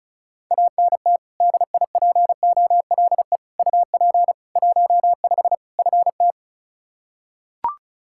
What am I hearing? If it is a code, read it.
Morse code, heard as a keyed tone, about 34 wpm, 700 Hz. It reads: ANT DIPOLE UP 15 FT
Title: Practice QSO - Exchange 2 - Info - 35wpm